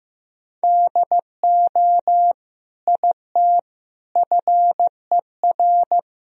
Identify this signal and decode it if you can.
Morse code, heard as a keyed tone, about 16 wpm, 700 Hz. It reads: DO IT FER
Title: Sentences from Top 200 Words and Abbreviations 15wpm